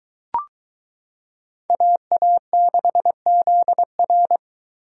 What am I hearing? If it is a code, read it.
Morse code, heard as a keyed tone, about 23 wpm, 700 Hz. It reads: AA6ZR